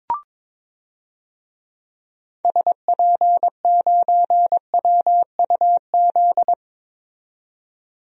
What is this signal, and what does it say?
Morse code, heard as a keyed tone, about 22 wpm, 700 Hz. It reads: SP9WUZ
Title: DXCC-100 Call Signs - 6 Characters - 22wpm